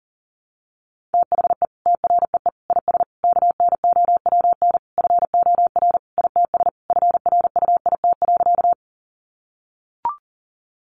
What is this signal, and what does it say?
Morse code, heard as a keyed tone, about 40 wpm, 700 Hz. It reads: THE TREE IS KNOWN FOR ITS FRUIT.
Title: Sentences from Top 1300 Words 40wpm